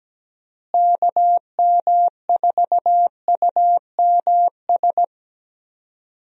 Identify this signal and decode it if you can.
Morse code, heard as a keyed tone, about 17 wpm, 700 Hz. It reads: KM4UMS
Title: USA Call Signs - 6 Characters - Repeated 2x - 17wpm